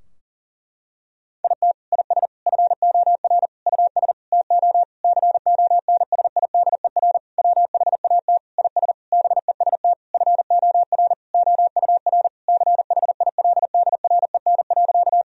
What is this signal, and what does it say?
Morse code, heard as a keyed tone, about 37 wpm, 700 Hz. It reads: IT IS FOR US TO CONSIDER WHAT IS BEST FOR OUR CHILDREN.
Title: Sentences from Top 1000 Words - Review Part 4 - 40wpm